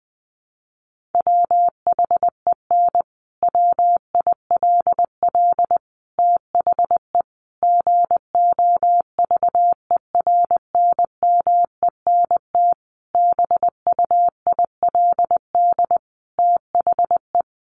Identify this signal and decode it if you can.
Morse code, heard as a keyed tone, about 20 wpm, 700 Hz. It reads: WHEN WILL THE GOVERNMENT BUILD THE
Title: Sentences from Top 1400 Words 20wpm